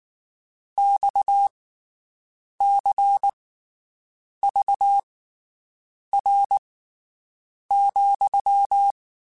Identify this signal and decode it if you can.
Morse code, heard as a keyed tone, about 19 wpm, spaced as for 7 wpm, 800 Hz. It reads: XCVR,
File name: QSOs Spaced